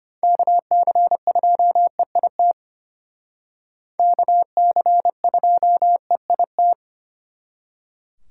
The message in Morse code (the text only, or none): KC2EIT KC2EIT